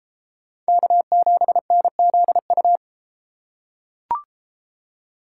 Morse code, heard as a keyed tone, about 33 wpm, 700 Hz. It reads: K7NZU